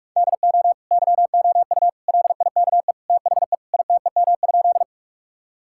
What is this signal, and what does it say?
Morse code, heard as a keyed tone, about 44 wpm, 700 Hz. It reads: DO YOU LIKE THE ITEM?
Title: Sentences from Top 1000 Words - Review Part 6 - 45wpm